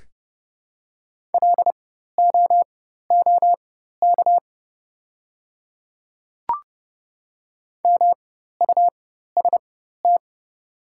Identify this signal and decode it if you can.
Morse code, heard as a keyed tone, about 32 wpm, spaced as for 14 wpm, 700 Hz. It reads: LOOK MUST
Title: 4x Character Spacing - Top 300 Words - 30wpm